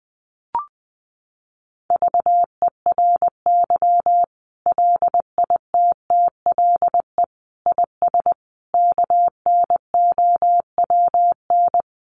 Morse code, heard as a keyed tone, about 20 wpm, 700 Hz. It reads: VERY LITTLE IS KNOWN
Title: Sentences from Top 1500 Words 20wpm